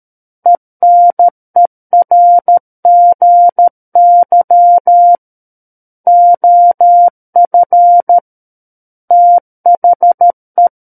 Morse code, heard as a keyed tone, about 13 wpm, 700 Hz. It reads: ENERGY OF THE
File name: Book One / 04-Chapter 4 - 13 WPM